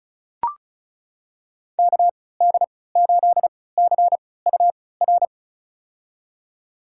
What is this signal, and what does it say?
Morse code, heard as a keyed tone, about 36 wpm, spaced as for 20 wpm, 700 Hz. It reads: KD8CUR
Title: USA Call Signs - 3x Character Spacing - 35wpm